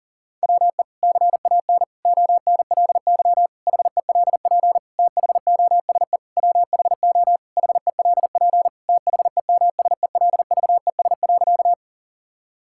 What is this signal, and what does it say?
Morse code, heard as a keyed tone, about 40 wpm, 700 Hz. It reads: WE CAN ONLY HELP THOSE WHO HELP THEMSELVES.
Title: Sentences from Top 1000 Words - Review Part 5 - 40wpm